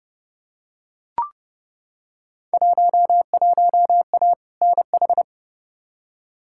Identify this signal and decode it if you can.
Morse code, heard as a keyed tone, about 31 wpm, 700 Hz. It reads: 11A NH
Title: Field Day Exchanges Encore - 30wpm